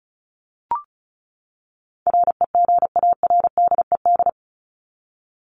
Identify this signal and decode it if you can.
Morse code, heard as a keyed tone, about 34 wpm, 700 Hz. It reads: REGARDED
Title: ED Suffix - Top 100 Words with Base Words 35wpm